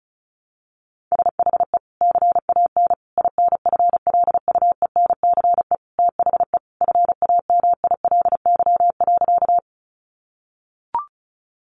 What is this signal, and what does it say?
Morse code, heard as a keyed tone, about 34 wpm, 700 Hz. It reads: SHE CAN INFLUENCE THE FAMILY.
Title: Sentences from Top 1100 Words 35wpm